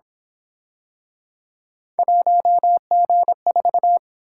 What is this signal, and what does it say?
Morse code, heard as a keyed tone, about 27 wpm, 700 Hz. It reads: 1G4
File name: Speed-Racing - Three Letter-Number Combinations - 22wpm